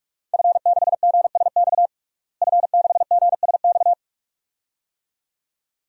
Morse code, heard as a keyed tone, about 46 wpm, 700 Hz. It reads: F6GSX F6GSX
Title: DXCC-100 Call Signs - 5 Characters - Repeated 2x - 45wpm